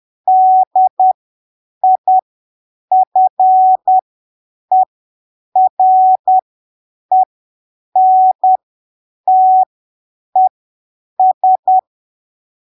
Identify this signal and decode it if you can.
Morse code, heard as a keyed tone, about 10 wpm, spaced as for 7 wpm, 750 Hz. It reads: DIFERENTES
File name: arquivos / texto / texto 12 10ppm